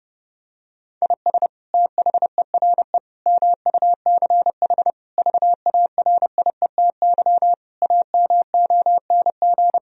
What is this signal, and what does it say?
Morse code, heard as a keyed tone, about 30 wpm, 700 Hz. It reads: IS THERE MUCH VARIETY AMONG
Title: Sentences from Top 1000 Words - Review Part 4 - 30wpm